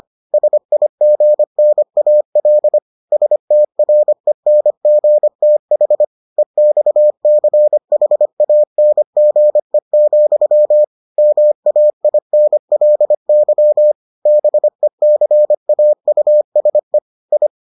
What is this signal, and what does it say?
Morse code, heard as a keyed tone, about 25 wpm, 600 Hz. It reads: SIGNAL STRENGTH EXCHANGE, MAINLY BECAUSE I